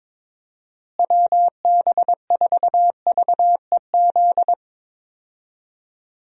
Morse code, heard as a keyed tone, about 22 wpm, 700 Hz. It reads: WB4VEZ
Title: USA Call Signs - 6 Characters - Repeated 2x - 22wpm